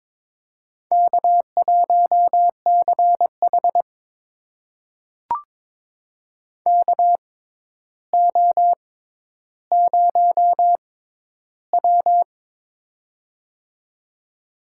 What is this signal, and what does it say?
Morse code, heard as a keyed tone, about 22 wpm, spaced as for 11 wpm, 700 Hz. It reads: K1CH KO0W
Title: USA Call Signs - 4 Characters - 6x Character Spacing - 22wpm